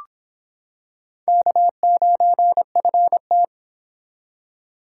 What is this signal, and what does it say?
Morse code, heard as a keyed tone, about 26 wpm, 700 Hz. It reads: K9FT